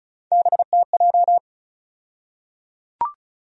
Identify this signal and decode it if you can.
Morse code, heard as a keyed tone, about 36 wpm, 700 Hz. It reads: BTJ
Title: USA Call Sign Postfixes - 3 Letters 35wpm